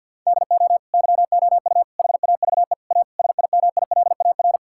Morse code, heard as a keyed tone, about 50 wpm, 700 Hz. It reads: DO YOU HAVE A SIMILAR
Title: Sentences from Top 1000 Words - Review Part 5 - 50wpm